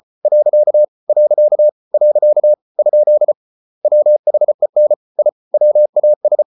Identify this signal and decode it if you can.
Morse code, heard as a keyed tone, about 34 wpm, 600 Hz. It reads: . . ? WHEN I WAS